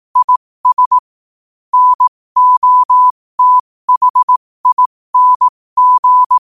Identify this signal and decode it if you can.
Morse code, heard as a keyed tone, about 18 wpm, spaced as for 15 wpm, 1000 Hz. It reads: IS NOTHING
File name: qotdfiles / 15wpm